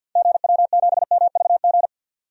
Morse code, heard as a keyed tone, about 52 wpm, 700 Hz. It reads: GW7MVG
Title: DXCC-100 Call Signs - 6 Characters - 50wpm